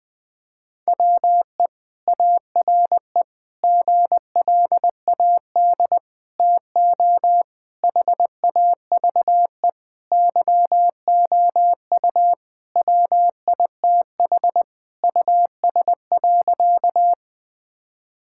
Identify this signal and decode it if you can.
Morse code, heard as a keyed tone, about 21 wpm, 700 Hz. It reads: WE ARE GLAD TO HAVE YOU WITH US.
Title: Sentences with 5-Word Phrases - 20wpm